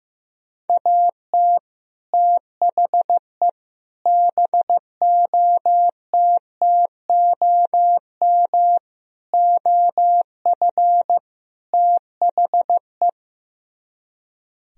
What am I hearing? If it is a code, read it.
Morse code, heard as a keyed tone, about 15 wpm, 700 Hz. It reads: AT THE BOTTOM OF THE
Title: Speed-Racing - Sets of 5 Words 15wpm